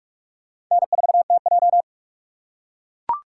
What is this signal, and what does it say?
Morse code, heard as a keyed tone, about 41 wpm, 690 Hz. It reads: N4TJ